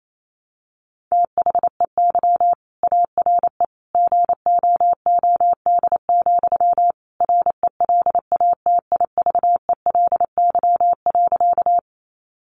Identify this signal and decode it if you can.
Morse code, heard as a keyed tone, about 28 wpm, 700 Hz. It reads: THEY ARE GOOD, RELATIVELY.
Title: Sentences from Top 1100 Words 28wpm